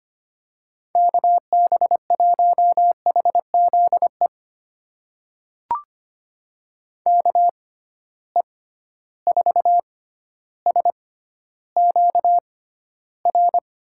Morse code, heard as a keyed tone, about 25 wpm, spaced as for 13 wpm, 700 Hz. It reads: KB1HZE KE4SQR
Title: USA Call Signs - 6 Characters - 6x Character Spacing - 25wpm